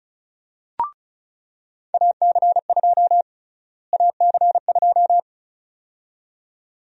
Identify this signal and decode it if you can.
Morse code, heard as a keyed tone, about 34 wpm, 700 Hz. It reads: AC2 AC2
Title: USA Call Sign Prefixes - Repeated 2x - 35wpm